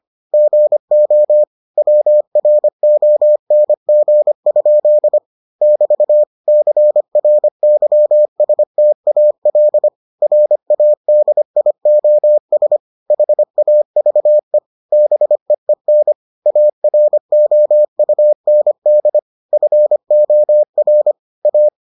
Morse code, heard as a keyed tone, about 25 wpm, 600 Hz. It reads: GO WRONG? <BT> CRYSTAL RADIOS HAVE BEEN AROUND FOR A